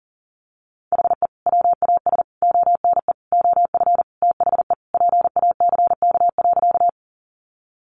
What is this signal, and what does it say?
Morse code, heard as a keyed tone, about 37 wpm, 700 Hz. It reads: HE WAS ONE OF THE PACK.
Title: Sentences from Top 1300 Words 40wpm